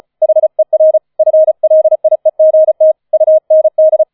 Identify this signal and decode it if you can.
Morse code, heard as a keyed tone, about 34 wpm, 630 Hz. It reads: HER FLIEGT UND
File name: 005 Fünfter Streich 35wpm